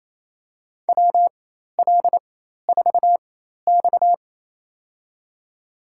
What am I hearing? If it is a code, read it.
Morse code, heard as a keyed tone, about 28 wpm, spaced as for 13 wpm, 700 Hz. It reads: WL4X
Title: USA Call Signs - 4 Characters - 4x Character Spacing - 28wpm